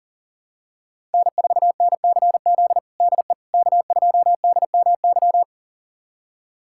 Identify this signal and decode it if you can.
Morse code, heard as a keyed tone, about 40 wpm, 700 Hz. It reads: N4NCZ DE K2DMY
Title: USA Callsign DE Callsign - 5 Characters - 40wpm